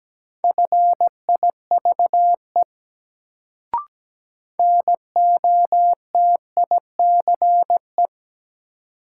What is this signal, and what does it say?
Morse code, heard as a keyed tone, about 17 wpm, 700 Hz. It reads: FIVE NOTICE
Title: Sentences from Top 400 Words - New Word Drill - 17wpm